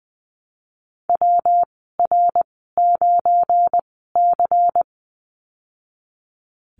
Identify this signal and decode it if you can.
Morse code, heard as a keyed tone, about 19 wpm, spaced as for 14 wpm, 700 Hz. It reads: WR9C
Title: 193 Lesson 31g - Letter v - Call Signs 20wpm